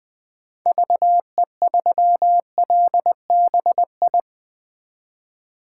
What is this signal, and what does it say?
Morse code, heard as a keyed tone, about 20 wpm, 700 Hz. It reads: VE3LBI
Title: 2023 POTA Hunter Call Signs - 20wpm